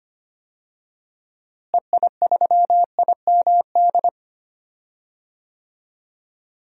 Morse code, heard as a keyed tone, about 25 wpm, 700 Hz. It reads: EI3IMD
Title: DXCC-100 Call Signs - 6 Characters - Repeated 2x - 25wpm